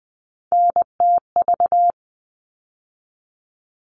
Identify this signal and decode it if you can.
Morse code, heard as a keyed tone, about 20 wpm, 700 Hz. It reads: NTV
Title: All Cut Numbers - 3-Digit 20wpm